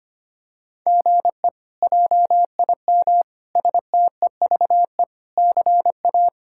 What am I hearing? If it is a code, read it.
Morse code, heard as a keyed tone, about 25 wpm, 700 Hz. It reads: GE JIM STEVE CA
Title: K1USN SST QSO Practice - 25wpm